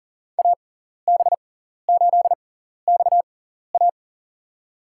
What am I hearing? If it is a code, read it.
Morse code, heard as a keyed tone, about 40 wpm, spaced as for 14 wpm, 690 Hz. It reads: AB8XA
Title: USA Call Signs - 6x Character Spacing - 40wpm